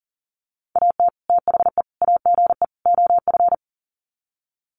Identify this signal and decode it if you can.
Morse code, heard as a keyed tone, about 37 wpm, 700 Hz. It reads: AT THE AGE OF
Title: Sets of 4 Words 40wpm